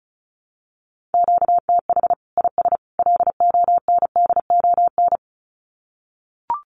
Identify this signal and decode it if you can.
Morse code, heard as a keyed tone, about 34 wpm, 700 Hz. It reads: QTH IS LONDON
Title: Common QSO Phrases 35wpm